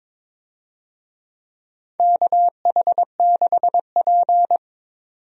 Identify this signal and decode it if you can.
Morse code, heard as a keyed tone, about 23 wpm, 700 Hz. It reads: KH6P
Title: DXCC-100 Call Signs - 4 Characters - 22wpm